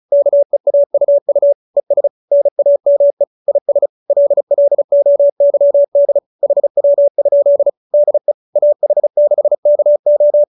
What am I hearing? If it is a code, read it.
Morse code, heard as a keyed tone, about 34 wpm, 570 Hz. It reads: KEAUU ES NAME IS LLOYD HW? DE AH6KO